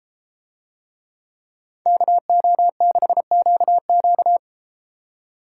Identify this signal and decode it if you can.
Morse code, heard as a keyed tone, about 31 wpm, 700 Hz. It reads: KO6QQ